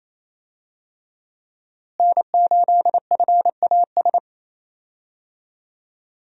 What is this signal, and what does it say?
Morse code, heard as a keyed tone, about 28 wpm, 700 Hz. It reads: N8FAS
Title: USA Call Signs - 5 Characters - Repeated 2x - 28wpm